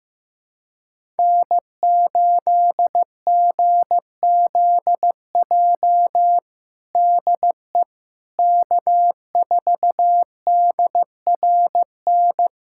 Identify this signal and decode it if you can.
Morse code, heard as a keyed tone, about 15 wpm, 700 Hz. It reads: N8GZJ DE K4DRN
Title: USA Callsign DE Callsign 15wpm